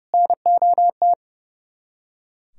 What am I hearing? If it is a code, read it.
Morse code, heard as a keyed tone, about 38 wpm, 700 Hz. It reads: NOT